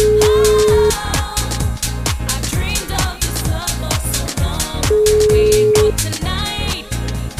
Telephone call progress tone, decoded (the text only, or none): ringback tone